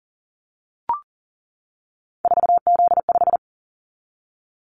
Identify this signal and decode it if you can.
Morse code, heard as a keyed tone, about 38 wpm, 700 Hz. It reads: 4Z5
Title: Callsign Prefixes for Top 50 DXCC - 40wpm